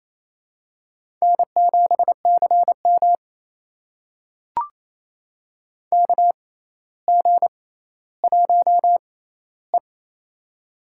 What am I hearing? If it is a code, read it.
Morse code, heard as a keyed tone, about 28 wpm, spaced as for 14 wpm, 700 Hz. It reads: N7CM KG1E
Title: USA Call Signs - 6x Character Spacing - 28wpm